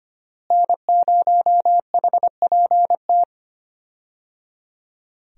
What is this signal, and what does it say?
Morse code, heard as a keyed tone, about 25 wpm, 700 Hz. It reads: N0HPT
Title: USA Call Signs - 5 Characters - Repeated 3x - 25wpm